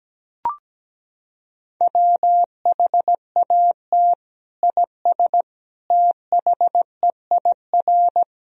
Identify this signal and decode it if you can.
Morse code, heard as a keyed tone, about 17 wpm, 700 Hz. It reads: WHAT IS THEIR
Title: Sentences from Top 1000 Words - Review Part 1 - 17wpm